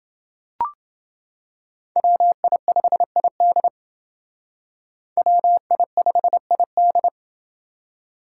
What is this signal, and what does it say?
Morse code, heard as a keyed tone, about 28 wpm, 700 Hz. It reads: WI5ID WI5ID